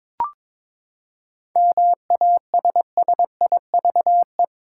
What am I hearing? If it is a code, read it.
Morse code, heard as a keyed tone, about 22 wpm, 700 Hz. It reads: MASSIVE
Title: Seven Letter Words - 22wpm